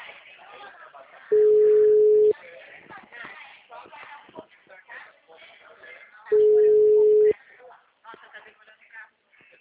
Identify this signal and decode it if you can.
Ringback tone (Europe, 425 Hz), with speech.
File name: out- -2005- - - .260